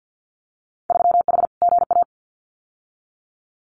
Morse code, heard as a keyed tone, about 50 wpm, 700 Hz. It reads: 3H GA
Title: Winter Field Day Exchanges Encore 50wpm